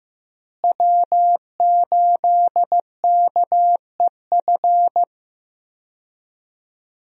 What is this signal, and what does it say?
Morse code, heard as a keyed tone, about 15 wpm, 700 Hz. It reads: W8KEF
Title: USA Call Signs - 5 Characters - Repeated 2x - 15wpm